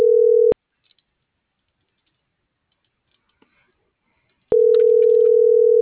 Ringback tone (North America), 440 + 480 Hz.